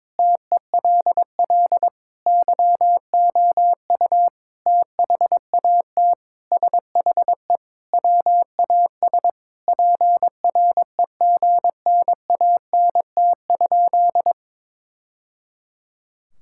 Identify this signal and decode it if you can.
Morse code, heard as a keyed tone, about 22 wpm, 700 Hz. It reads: TELL YOU THAT SHE WAS PREGNANT?
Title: Sentences from Top 1000 Words - Review Part 6 - 22wpm